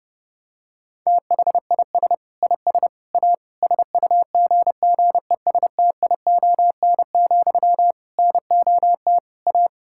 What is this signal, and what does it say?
Morse code, heard as a keyed tone, about 32 wpm, 700 Hz. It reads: THIS IS A SUGGESTION, NOT A
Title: Sentences from Top 1600 Words 30wpm